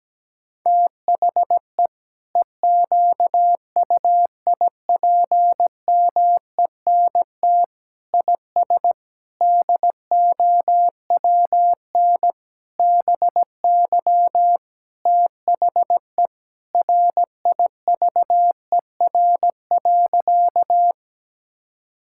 Morse code, heard as a keyed tone, about 17 wpm, 700 Hz. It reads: THE EQUIPMENT IS DOWN BY THE RIVER.
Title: Sentences from Top 1000 Words - Review Part 5 - 17wpm